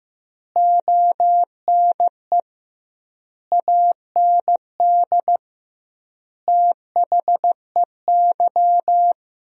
Morse code, heard as a keyed tone, about 15 wpm, 700 Hz. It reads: ONE AND THEY
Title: 2x Word-Spacing - Sentences from Top 300 Words - 15wpm